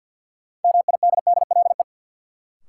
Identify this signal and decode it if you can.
Morse code, heard as a keyed tone, about 53 wpm, 700 Hz. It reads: MIDDLE
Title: Sets of 2 Words - New Word Drill - 50wpm